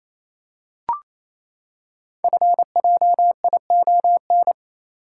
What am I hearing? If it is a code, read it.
Morse code, heard as a keyed tone, about 27 wpm, 700 Hz. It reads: FJION
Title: Code Groups - 5 Letters - 28wpm